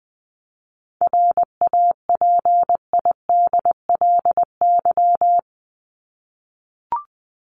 Morse code, heard as a keyed tone, about 20 wpm, 700 Hz. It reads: RAPIDLY
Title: LY Suffix - Top 100 Words with Base Words 20wpm